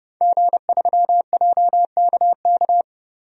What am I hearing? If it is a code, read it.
Morse code, heard as a keyed tone, about 30 wpm, 700 Hz. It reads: G3JKK
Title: DXCC-100 Call Signs - 5 Characters - 30wpm